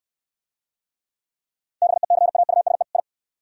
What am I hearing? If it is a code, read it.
Morse code, heard as a keyed tone, about 62 wpm, 700 Hz. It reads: BECAUSE I